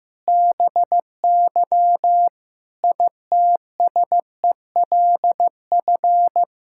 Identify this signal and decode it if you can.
Morse code, heard as a keyed tone, about 15 wpm, 700 Hz. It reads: BY ITSELF
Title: Sentences from Top 1000 Words - Review Part 5 - 15wpm